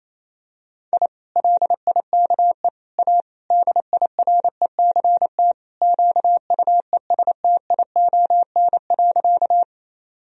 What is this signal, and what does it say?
Morse code, heard as a keyed tone, about 27 wpm, 700 Hz. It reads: I LIKE A DIRECT QUESTION.
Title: Sentences from Top 1000 Words - Review Part 2 - 28wpm